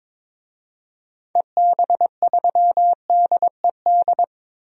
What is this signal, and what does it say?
Morse code, heard as a keyed tone, about 21 wpm, 700 Hz. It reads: EB3DED
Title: DXCC-100 Call Signs - 6 Characters - 22wpm